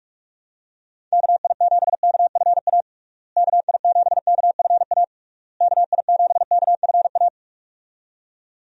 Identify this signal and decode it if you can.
Morse code, heard as a keyed tone, about 46 wpm, 700 Hz. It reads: KI7KFA KI7KFA KI7KFA
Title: USA Call Signs - 6 Characters - Repeated 3x - 45wpm